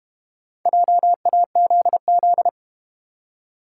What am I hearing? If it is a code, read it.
Morse code, heard as a keyed tone, about 32 wpm, 700 Hz. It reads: JAZZ